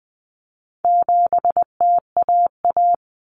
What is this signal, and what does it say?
Morse code, heard as a keyed tone, about 20 wpm, 700 Hz. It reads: 7TAA
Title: Common Cut Numbers - 4-Digit 20wpm